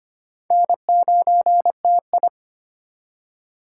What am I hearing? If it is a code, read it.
Morse code, heard as a keyed tone, about 35 wpm, spaced as for 30 wpm, 700 Hz. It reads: N9TI
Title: Speed Racing - USA 1x2 Call Signs 25wpm